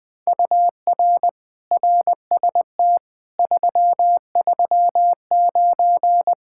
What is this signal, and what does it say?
Morse code, heard as a keyed tone, about 21 wpm, 700 Hz. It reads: UR RST 339
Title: Speed-Racing Common QSO Phrases - 20wpm